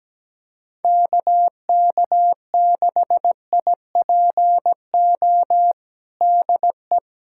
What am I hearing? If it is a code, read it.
Morse code, heard as a keyed tone, about 17 wpm, 700 Hz. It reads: KK6IPO DE
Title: USA Callsign DE Callsign 17wpm